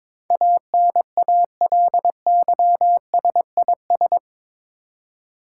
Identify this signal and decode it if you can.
Morse code, heard as a keyed tone, about 22 wpm, 700 Hz. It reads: ANALYSIS
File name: practice / 22wpm / Eight Letter Words - 22wpm